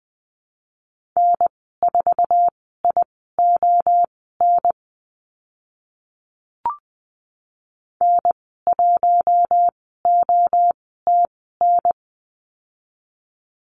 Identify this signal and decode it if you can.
Morse code, heard as a keyed tone, about 20 wpm, spaced as for 14 wpm, 700 Hz. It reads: N4ION N1OTN
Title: 048 Lesson 9g - Number 4 - Call Signs 20wpm